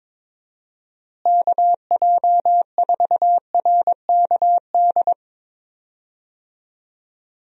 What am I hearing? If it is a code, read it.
Morse code, heard as a keyed tone, about 22 wpm, 700 Hz. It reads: KJ4RKD
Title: USA Call Signs - 6 Characters - Repeated 3x - 22wpm